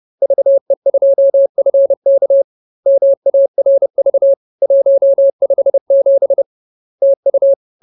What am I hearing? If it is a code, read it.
Morse code, heard as a keyed tone, about 30 wpm, 550 Hz. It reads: VE2FK MARV 157 TU